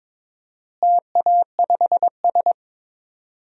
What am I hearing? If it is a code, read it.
Morse code, heard as a keyed tone, about 22 wpm, 700 Hz. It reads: TA5S
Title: DXCC-100 Call Signs - 4 Characters - Repeated 2x - 22wpm